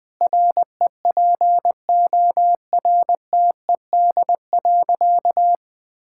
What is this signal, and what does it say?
Morse code, heard as a keyed tone, about 21 wpm, 700 Hz. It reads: REPORTED.
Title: Sentences with 3-Word Phrases - 20wpm